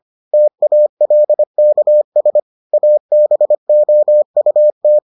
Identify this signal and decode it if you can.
Morse code, heard as a keyed tone, about 25 wpm, 600 Hz. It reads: TALKS ABOUT